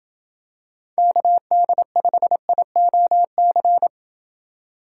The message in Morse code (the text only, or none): KD5IOC